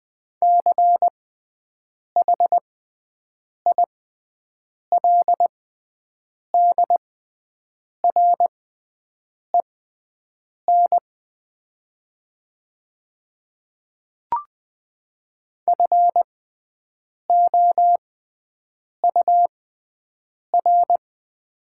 Morse code, heard as a keyed tone, about 20 wpm, spaced as for 7 wpm, 700 Hz. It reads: CHILDREN FOUR
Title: 6x Character Spacing - Top 500 Words - 20wpm